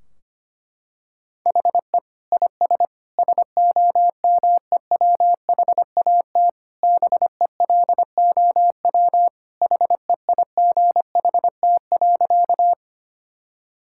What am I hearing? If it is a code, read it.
Morse code, heard as a keyed tone, about 25 wpm, 700 Hz. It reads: HE IS SOMEWHAT BELOW HEIGHT.
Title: Sentences from Top 1600 Words 25wpm